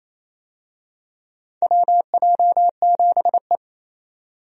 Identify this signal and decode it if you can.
Morse code, heard as a keyed tone, about 28 wpm, 700 Hz. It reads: WJ7E
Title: USA Call Signs - 4 Characters - 28wpm